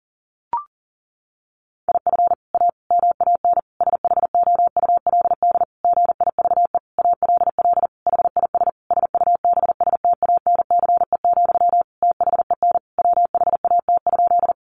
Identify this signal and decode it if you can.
Morse code, heard as a keyed tone, about 37 wpm, 700 Hz. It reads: IF A MAN SHOULD GIVE ALL HIS SUBSTANCE, THEN WHAT?
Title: Sentences from Top 1500 Words 40wpm